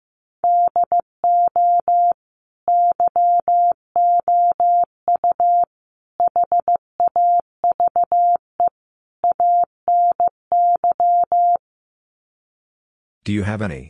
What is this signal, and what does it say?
Morse code, heard as a keyed tone, about 15 wpm, 700 Hz. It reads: DO YOU HAVE ANY
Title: Sets of 4 Words 15wpm